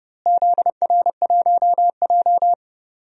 Morse code, heard as a keyed tone, about 32 wpm, 700 Hz. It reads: ZR1J
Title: DXCC-100 Call Signs - 4 Characters - Repeated 2x - 30wpm